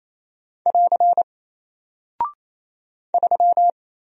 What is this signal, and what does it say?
Morse code, heard as a keyed tone, about 27 wpm, 700 Hz. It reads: <AR> 3
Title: Single Character (Letter:Number:Punctuation:Prosign) 28wpm